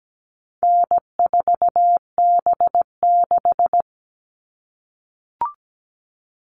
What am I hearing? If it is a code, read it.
Morse code, heard as a keyed tone, about 17 wpm, 700 Hz. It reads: N4B6
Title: All Cut Numbers - 4-Digit 17wpm